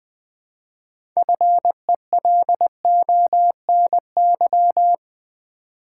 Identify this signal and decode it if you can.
Morse code, heard as a keyed tone, about 20 wpm, 700 Hz. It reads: FELONY
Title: 6 Letter Words - First 3 Letters Given - 20wpm